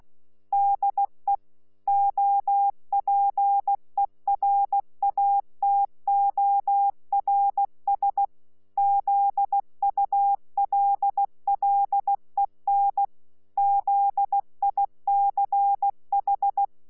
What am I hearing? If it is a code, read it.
Morse code, heard as a keyed tone, about 16 wpm, 800 Hz. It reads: DE OPERATORS ZULLEN ZICH
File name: Z PI4VRZ / PI4VRZ 14-jun-2025